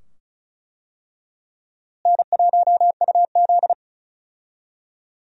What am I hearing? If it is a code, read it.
Morse code, heard as a keyed tone, about 34 wpm, 700 Hz. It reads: N1UZ
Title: USA Call Signs - 4 Characters - Repeated 2x - 35wpm